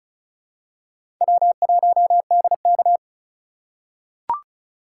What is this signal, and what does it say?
Morse code, heard as a keyed tone, about 36 wpm, 690 Hz. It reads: W1DK
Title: USA Call Signs - 4 Characters - 2x Character Spacing - 35wpm